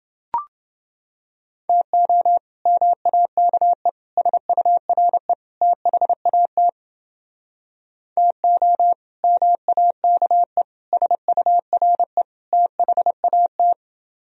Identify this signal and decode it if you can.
Morse code, heard as a keyed tone, about 29 wpm, 700 Hz. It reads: TO MAKE SURE THAT TO MAKE SURE THAT